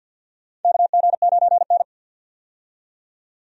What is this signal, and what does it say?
Morse code, heard as a keyed tone, about 48 wpm, 700 Hz. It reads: KG9N
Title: USA Call Signs - 4 Characters - 50wpm